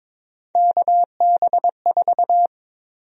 Morse code, heard as a keyed tone, about 22 wpm, 700 Hz. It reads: KB4
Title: USA Call Sign Prefixes - 22wpm